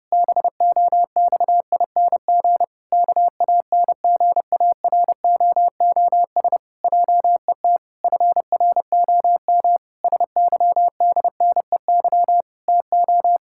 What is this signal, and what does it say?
Morse code, heard as a keyed tone, about 30 wpm, 700 Hz. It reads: BOXING KANGAROOS JET FROM SYDNEY TO
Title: Pangram Challenge v2 - 30wpm